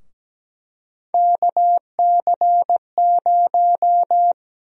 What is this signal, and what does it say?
Morse code, heard as a keyed tone, about 17 wpm, 700 Hz. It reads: KC0
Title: USA Call Sign Prefixes - 17wpm